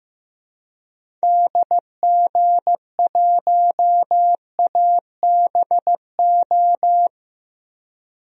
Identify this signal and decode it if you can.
Morse code, heard as a keyed tone, about 15 wpm, 700 Hz. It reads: DG1ABO
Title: DXCC-100 Call Signs - 6 Characters - Repeated 3x - 15wpm